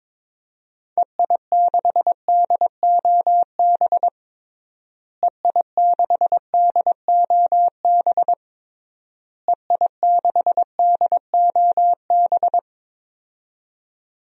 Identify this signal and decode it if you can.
Morse code, heard as a keyed tone, about 22 wpm, 700 Hz. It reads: EI6DOB EI6DOB EI6DOB
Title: DXCC-100 Call Signs - 6 Characters - Repeated 3x - 22wpm